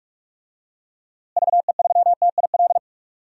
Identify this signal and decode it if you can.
Morse code, heard as a keyed tone, about 46 wpm, 700 Hz. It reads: VE3TIL
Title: DXCC-100 Call Signs - 6 Characters - Repeated 3x - 45wpm